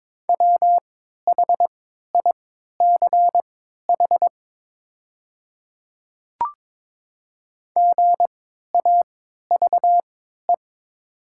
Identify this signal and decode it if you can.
Morse code, heard as a keyed tone, about 23 wpm, spaced as for 13 wpm, 700 Hz. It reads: WHICH GAVE
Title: 3x Character Spacing - Top 500 Words - 22wpm